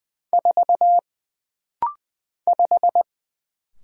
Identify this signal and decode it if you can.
Morse code, heard as a keyed tone, about 20 wpm, 700 Hz. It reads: 4 5
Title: Single Letter-Number Alt - Rapid-Fire 20wpm